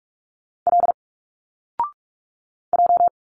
Morse code, heard as a keyed tone, about 46 wpm, 700 Hz. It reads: L J
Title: Single Letters 45wpm